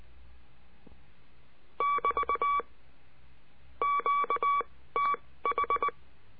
Morse code, heard as a keyed tone, about 19 wpm, spaced as for 15 wpm, 1140 Hz. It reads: <BT> QTH
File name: CURSO DE TELEGRAFIA